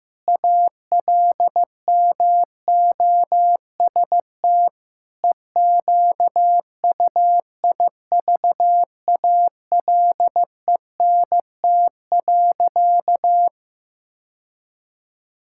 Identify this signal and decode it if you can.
Morse code, heard as a keyed tone, about 15 wpm, 700 Hz. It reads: ALMOST EQUIVALENT.
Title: Sentences from Top 1600 Words 15wpm